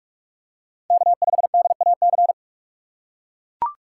Morse code, heard as a keyed tone, about 41 wpm, 700 Hz. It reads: K5DAC
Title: USA Call Signs - 5 Characters - 3x Character Spacing - 45wpm